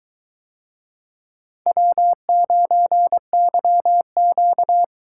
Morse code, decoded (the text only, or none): W9YQ